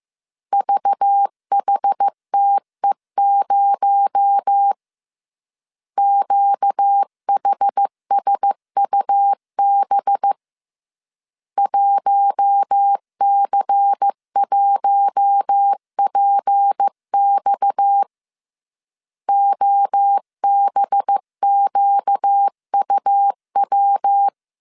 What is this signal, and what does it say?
Morse code, heard as a keyed tone, about 15 wpm, 800 Hz. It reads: VHTE0 QHSUB 1C1PX OBQUW
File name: CW / DET / 1 6 1500C S A 14wpm 2921